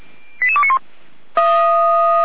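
Five-tone selective call (ZVEI), heard as quick runs of five tones, tones of about 70 ms. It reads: 88291
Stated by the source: ZVEI five-tone selective call received on an SDR, fire brigade Feuerwehr Orlamünde.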